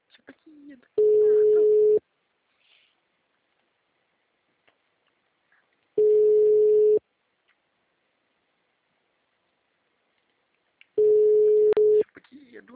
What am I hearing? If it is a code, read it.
Ringback tone (Europe, 425 Hz), with speech.